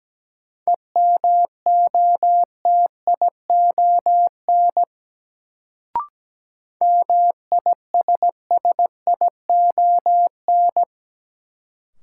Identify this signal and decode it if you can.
Morse code, heard as a keyed tone, about 17 wpm, 700 Hz. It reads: EMOTION MISSION
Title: Sentences from Top 1600 Words - New Word Drill Alt - 17wpm